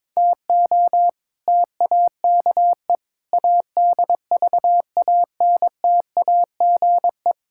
Morse code, heard as a keyed tone, about 22 wpm, 700 Hz. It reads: TO TAKE ADVANTAGE
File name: practice / 22wpm / Build A Phrase - 5-Word Phrases - 22wpm